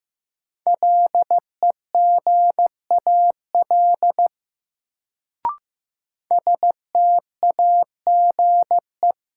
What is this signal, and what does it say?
Morse code, heard as a keyed tone, about 15 wpm, 700 Hz. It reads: LEGAL STAGE
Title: Sentences from Top 1200 Words - New Word Drill Alt - 15wpm